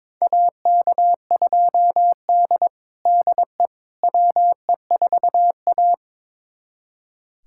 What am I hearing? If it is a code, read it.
Morse code, heard as a keyed tone, about 21 wpm, 700 Hz. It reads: AK2D DE WE4A
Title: USA Callsign DE Callsign - 4 Characters - 22wpm